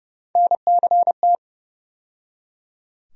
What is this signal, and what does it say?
Morse code, heard as a keyed tone, about 29 wpm, 700 Hz. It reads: NCT
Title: USA Call Sign Postfixes - 3 Letters 30wpm